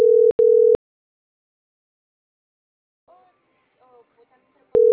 Ringback tone (North America), 440 + 480 Hz.